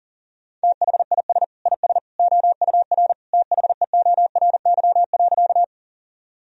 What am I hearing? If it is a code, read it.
Morse code, heard as a keyed tone, about 40 wpm, 700 Hz. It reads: THIS IS OUR THEORY.
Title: Sentences from Top 1000 Words - Review Part 3 - 40wpm